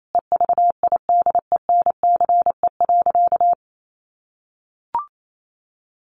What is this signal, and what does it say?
Morse code, heard as a keyed tone, about 28 wpm, 700 Hz. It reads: EVIDENCE.
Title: Sentences from Top 1300 Words 28wpm